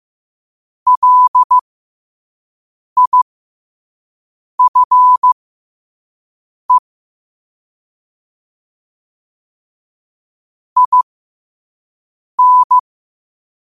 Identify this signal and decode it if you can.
Morse code, heard as a keyed tone, about 15 wpm, spaced as for 5 wpm, 1000 Hz. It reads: LIFE IN
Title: 5 WPM morse code quotes for Fri, 01 Mar 2024